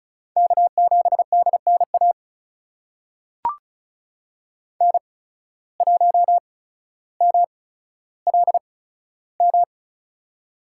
Morse code, heard as a keyed tone, about 34 wpm, spaced as for 15 wpm, 700 Hz. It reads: K7DNA N1MLM
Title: USA Call Signs - 5 Characters - 8x Character Spacing - 35wpm